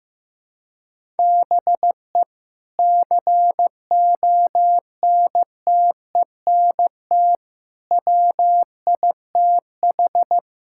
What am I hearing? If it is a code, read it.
Morse code, heard as a keyed tone, about 15 wpm, 700 Hz. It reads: BE CONTENT WITH
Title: Sentences from Top 1000 Words - Review Part 6 - 15wpm